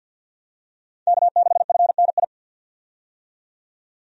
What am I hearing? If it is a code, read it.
Morse code, heard as a keyed tone, about 51 wpm, 700 Hz. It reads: K6FNI
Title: USA Call Signs - 5 Characters - 50wpm